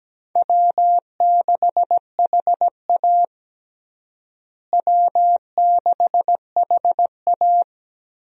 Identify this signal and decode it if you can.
Morse code, heard as a keyed tone, about 17 wpm, 700 Hz. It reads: W6HA W6HA
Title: USA Call Signs - 4 Characters - Repeated 3x - 17wpm